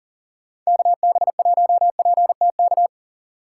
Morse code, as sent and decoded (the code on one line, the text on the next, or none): -.- -... .---- .--. - -.-
KB1PTK